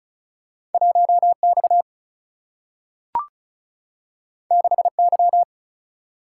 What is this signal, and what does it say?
Morse code, heard as a keyed tone, about 36 wpm, 700 Hz. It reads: 1 X 6 Y
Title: Single-Letter Single-Number 35wpm